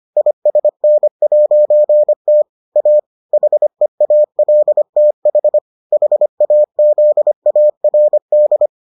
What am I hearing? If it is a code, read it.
Morse code, heard as a keyed tone, about 25 wpm, 600 Hz. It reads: ISN'T A HEALTH HAZARD